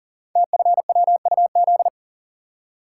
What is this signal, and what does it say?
Morse code, heard as a keyed tone, about 40 wpm, 700 Hz. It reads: TFWUZ
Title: Code Groups - 5 Letter:Numbers - 40wpm